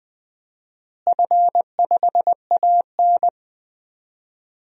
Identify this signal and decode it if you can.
Morse code, heard as a keyed tone, about 20 wpm, 700 Hz. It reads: F5AN
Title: DXCC-100 Call Signs - 4 Characters - Repeated 3x - 20wpm